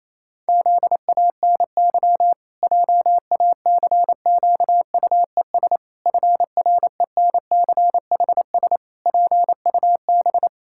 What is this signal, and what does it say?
Morse code, heard as a keyed tone, about 28 wpm, 700 Hz. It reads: ZANY JACQUES FRENCHS PUB
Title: Pangram Challenge v2 - 28wpm